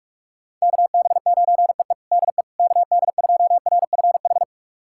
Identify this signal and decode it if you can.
Morse code, heard as a keyed tone, about 42 wpm, 700 Hz. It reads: KB9EE DE KD2RFH
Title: USA Callsign DE Callsign 45wpm